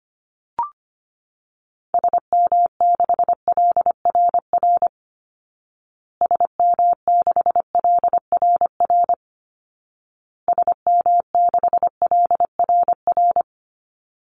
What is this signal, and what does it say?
Morse code, heard as a keyed tone, about 25 wpm, 700 Hz. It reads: SM6LRR SM6LRR SM6LRR
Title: Call Signs Repeated 2x - 25wpm